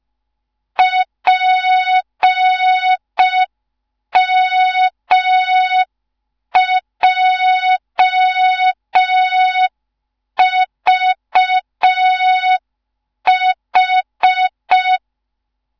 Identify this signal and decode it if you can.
Morse code, heard as a keyed tone, about 5 wpm, 3700 Hz. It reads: PMJVH
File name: Morze-HA2MN / hang / 15 GYAKORLAT